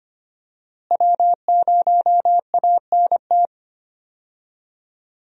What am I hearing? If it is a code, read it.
Morse code, heard as a keyed tone, about 25 wpm, 700 Hz. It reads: W0ANT
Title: USA Call Signs - 5 Characters - Repeated 2x - 25wpm